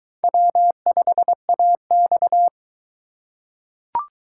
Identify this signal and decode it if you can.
Morse code, heard as a keyed tone, about 22 wpm, 700 Hz. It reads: W5AX